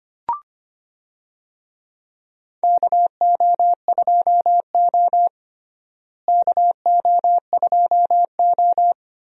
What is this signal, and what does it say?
Morse code, heard as a keyed tone, about 25 wpm, 700 Hz. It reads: KO2O KO2O
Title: USA Call Signs - 4 Characters - Repeated 3x - 25wpm